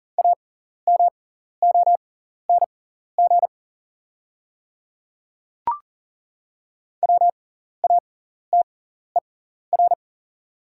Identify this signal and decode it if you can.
Morse code, heard as a keyed tone, about 40 wpm, spaced as for 14 wpm, 690 Hz. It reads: AMONG WATER
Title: 6x Character Spacing - Top 500 Words - 40wpm